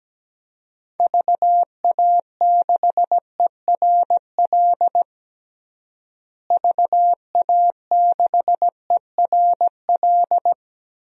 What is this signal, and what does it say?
Morse code, heard as a keyed tone, about 17 wpm, 700 Hz. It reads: VA6ERL VA6ERL
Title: DXCC-100 Call Signs - 6 Characters - Repeated 2x - 17wpm